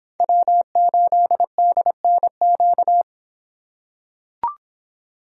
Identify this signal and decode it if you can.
Morse code, heard as a keyed tone, about 24 wpm, 700 Hz. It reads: W8DNQ